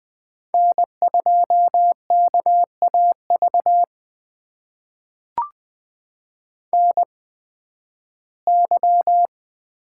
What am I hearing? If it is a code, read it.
Morse code, heard as a keyed tone, about 20 wpm, spaced as for 13 wpm, 700 Hz. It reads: N2KAV NY
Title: USA Call Signs - 8x Character Spacing - 20wpm